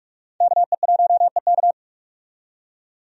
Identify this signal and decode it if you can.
Morse code, heard as a keyed tone, about 46 wpm, 700 Hz. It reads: KE1EK
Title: USA Call Signs - 5 Characters - 45wpm